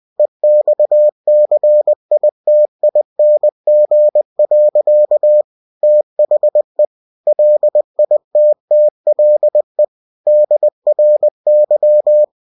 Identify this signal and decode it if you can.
Morse code, heard as a keyed tone, about 20 wpm, 600 Hz. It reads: EXCITING. THE LITTLE DRY
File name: cw-books / winnie-the-pooh / winnie-20wpm-page-0091